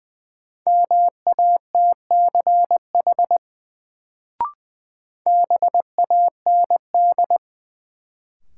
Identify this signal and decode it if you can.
Morse code, heard as a keyed tone, about 20 wpm, 700 Hz. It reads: MATCH BAND
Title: Sentences from Top 1600 Words - New Word Drill Alt - 20wpm